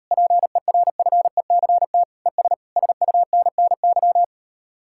Morse code, heard as a keyed tone, about 36 wpm, 700 Hz. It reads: PERFECT ES SUNNY